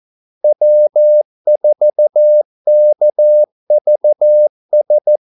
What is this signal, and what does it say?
Morse code, heard as a keyed tone, about 14 wpm, 600 Hz. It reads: W4KVS
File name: Callsigns-14wpm